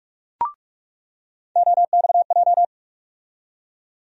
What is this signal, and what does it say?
Morse code, heard as a keyed tone, about 45 wpm, 700 Hz. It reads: OXJ
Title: USA Call Sign Postfixes - 3 Letters 45wpm